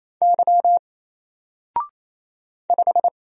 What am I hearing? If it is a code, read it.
Morse code, heard as a keyed tone, about 27 wpm, 700 Hz. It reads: Y 5
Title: Single Character (Letter:Number:Punctuation:Prosign) 28wpm